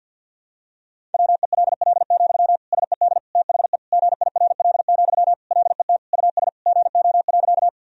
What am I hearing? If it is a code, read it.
Morse code, heard as a keyed tone, about 50 wpm, 700 Hz. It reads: WELL, SED THE GIRL, LET US GO.
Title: Sentences from Top 400 Words and Abbreviations 50wpm